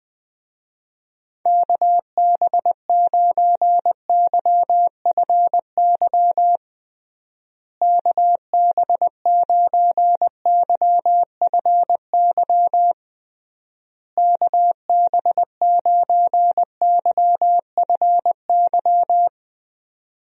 Morse code, heard as a keyed tone, about 21 wpm, 700 Hz. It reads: KB9YFY KB9YFY KB9YFY
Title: USA Call Signs - 6 Characters - Repeated 3x - 20wpm